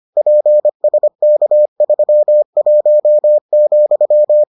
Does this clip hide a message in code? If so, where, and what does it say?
Yes: Morse code from 0.2 to 4.4 s, heard as a keyed tone, about 25 wpm, 600 Hz. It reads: PSK31,